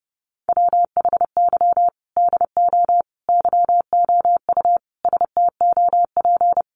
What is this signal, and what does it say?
Morse code, heard as a keyed tone, about 30 wpm, 700 Hz. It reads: WHY DO YOU STOP
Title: Sentences from Top 300 Words 30wpm